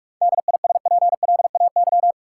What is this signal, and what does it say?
Morse code, heard as a keyed tone, about 46 wpm, 700 Hz. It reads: DISPLAY
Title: Sentences from Top 1200 Words - New Word Drill Alt - 45wpm